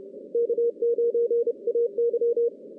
Morse code, heard as a keyed tone, about 31 wpm, 470 Hz. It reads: K9AY
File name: 2016 JAN CW NAQP / 2016 Jan naqp k5ot - 0-10